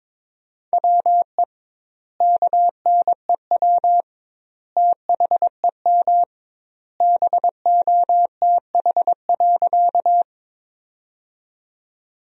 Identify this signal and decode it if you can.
Morse code, heard as a keyed tone, about 21 wpm, 700 Hz. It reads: WE KNEW THEM BOTH.
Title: 2x Word-Spacing - Sentences from Top 400 Words - 22wpm